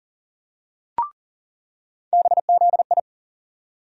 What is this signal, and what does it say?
Morse code, heard as a keyed tone, about 38 wpm, 700 Hz. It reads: BZI
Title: USA Call Sign Postfixes - 3 Letters 40wpm